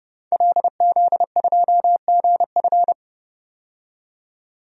Morse code, heard as a keyed tone, about 30 wpm, 700 Hz. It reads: LZ2GF
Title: DXCC-100 Call Signs - 5 Characters - 30wpm